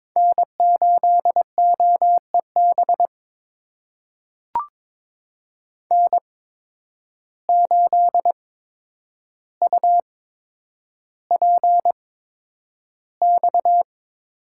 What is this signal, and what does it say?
Morse code, heard as a keyed tone, about 21 wpm, spaced as for 9 wpm, 700 Hz. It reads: N8OEB N8UPX
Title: USA Call Signs - 5 Characters - 8x Character Spacing - 22wpm